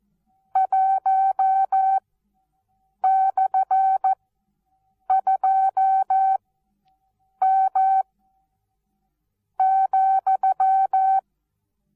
Morse code, heard as a keyed tone, about 14 wpm, spaced as for 6 wpm, 770 Hz. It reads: 1/2M,
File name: Onlinelehrgang / morsen / DJ4UF Morsen19